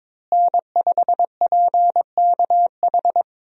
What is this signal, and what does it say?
Morse code, heard as a keyed tone, about 22 wpm, 700 Hz. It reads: N5PKH
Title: USA Call Signs - 5 Characters - Repeated 2x - 22wpm